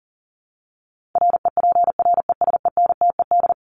Morse code, heard as a keyed tone, about 40 wpm, 700 Hz. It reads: REPRESENTED
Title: ED Suffix - Top 100 Words with Base Words 40wpm